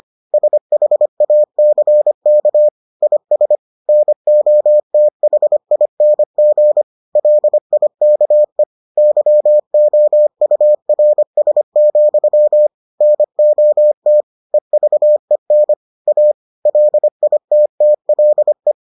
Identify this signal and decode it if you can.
Morse code, heard as a keyed tone, about 25 wpm, 600 Hz. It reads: SHACK IS NOTHING LIKE YOURS, NOT EVEN A LITTLE